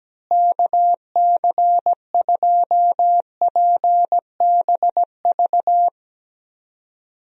Morse code, heard as a keyed tone, about 17 wpm, 700 Hz. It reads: KC2PBV
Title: USA Call Signs - 6 Characters - Repeated 3x - 17wpm